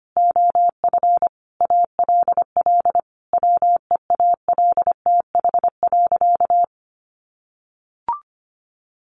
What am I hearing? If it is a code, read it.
Morse code, heard as a keyed tone, about 25 wpm, 700 Hz. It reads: OF ALL WEALTH.
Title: Sentences from Top 1200 Words 25wpm